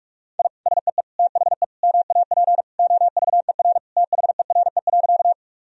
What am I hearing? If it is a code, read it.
Morse code, heard as a keyed tone, about 44 wpm, 700 Hz. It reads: I SEE THE MAP OVER THERE.
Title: Sentences from Top 1000 Words - Review Part 2 - 45wpm